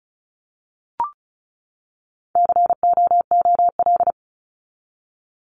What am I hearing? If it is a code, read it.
Morse code, heard as a keyed tone, about 34 wpm, 700 Hz. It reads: COOL
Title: Top 1001-1250 Words 35wpm